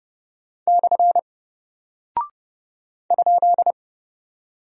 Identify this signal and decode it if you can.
Morse code, heard as a keyed tone, about 30 wpm, 700 Hz. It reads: / ?
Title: Single Character (Letter:Number:Punctuation:Prosign) Alt 30wpm